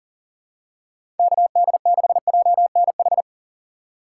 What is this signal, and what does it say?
Morse code, heard as a keyed tone, about 40 wpm, 700 Hz. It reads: KD6JNH
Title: USA Call Signs - 6 Characters - 2x Character Spacing - 40wpm